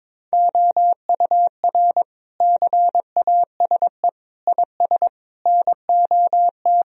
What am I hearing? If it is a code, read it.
Morse code, heard as a keyed tone, about 22 wpm, 700 Hz. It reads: OUR CASE IS NOT
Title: Sentences from Top 1000 Words - Review Part 2 - 22wpm